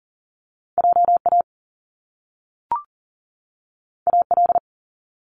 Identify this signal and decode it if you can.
Morse code, heard as a keyed tone, about 41 wpm, 700 Hz. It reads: JA AL
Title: Top Two Letter Combinations in Words 40wpm